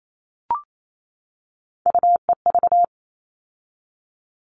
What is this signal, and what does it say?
Morse code, heard as a keyed tone, about 28 wpm, 700 Hz. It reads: UEV
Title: All Cut Numbers - 3-Digit 28wpm